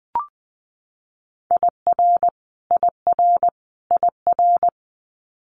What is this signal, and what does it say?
Morse code, heard as a keyed tone, about 20 wpm, 700 Hz. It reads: IR IR IR
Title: Multi-letter Phonemes Repeated 3x 20wpm